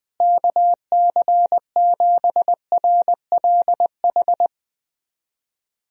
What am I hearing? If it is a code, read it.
Morse code, heard as a keyed tone, about 21 wpm, 700 Hz. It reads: KC7RLH
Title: USA Call Signs - 6 Characters - Repeated 3x - 20wpm